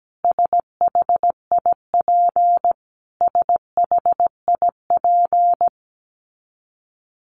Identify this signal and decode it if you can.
Morse code, heard as a keyed tone, about 17 wpm, 700 Hz. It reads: SHIP SHIP
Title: Top 500 Words Repeated 17wpm